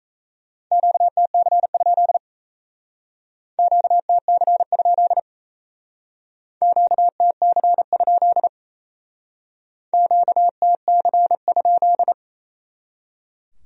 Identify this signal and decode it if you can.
Morse code, heard as a keyed tone, about 37 wpm, 700 Hz. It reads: QTC? QTC? QTC? QTC?